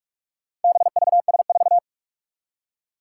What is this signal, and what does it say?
Morse code, heard as a keyed tone, about 41 wpm, 700 Hz. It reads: BVS4
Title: Four Challenging Letter-Number Combinations 45wpm